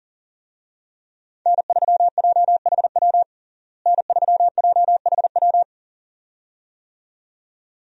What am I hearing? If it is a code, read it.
Morse code, heard as a keyed tone, about 40 wpm, 700 Hz. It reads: N3JHW N3JHW
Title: USA Call Signs - 5 Characters - Repeated 2x - 40wpm